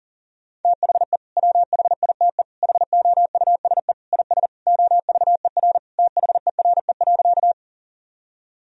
Morse code, heard as a keyed tone, about 37 wpm, 700 Hz. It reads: THE WHITE HOUSE IS OVER THERE.
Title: Sentences from Top 1000 Words - Review Part 1 - 40wpm